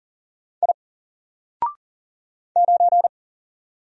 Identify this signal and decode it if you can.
Morse code, heard as a keyed tone, about 40 wpm, 700 Hz. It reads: I 9
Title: Single Letter-Number 40wpm